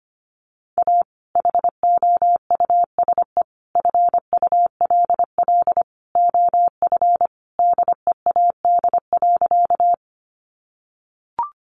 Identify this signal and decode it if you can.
Morse code, heard as a keyed tone, about 25 wpm, 700 Hz. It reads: A HOUSE FULL OF DEAD.
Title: Sentences from Top 1500 Words 25wpm